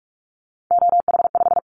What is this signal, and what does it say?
Morse code, heard as a keyed tone, about 44 wpm, 700 Hz. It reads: OH5
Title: Callsign Prefixes for Top 25 DXCC - Repeated 3x - 45wpm